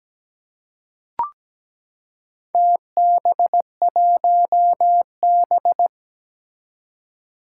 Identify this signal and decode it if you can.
Morse code, heard as a keyed tone, about 17 wpm, 700 Hz. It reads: TB1B
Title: Four Challenging Letter-Number Combinations 17wpm